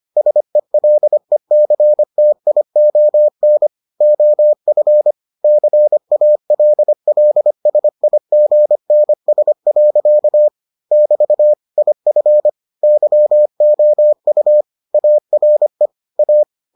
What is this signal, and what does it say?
Morse code, heard as a keyed tone, about 25 wpm, 600 Hz. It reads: SELECTION OF CALLSIGNS. <BT> IF YOU ARE A